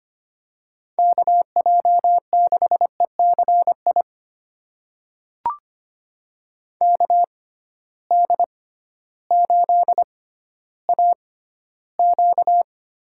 Morse code, heard as a keyed tone, about 25 wpm, spaced as for 14 wpm, 700 Hz. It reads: KJ6ECI KD8AQ
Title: USA Call Signs - 6 Characters - 6x Character Spacing - 25wpm